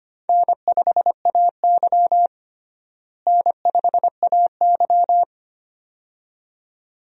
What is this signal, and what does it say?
Morse code, heard as a keyed tone, about 25 wpm, 700 Hz. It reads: N5AY N5AY
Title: USA Call Signs - 4 Characters - Repeated 2x - 25wpm